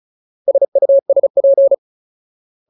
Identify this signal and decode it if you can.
Morse code, heard as a keyed tone, about 34 wpm, 550 Hz. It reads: SUSP
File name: Ultimate Word List / 35WPM Ultimate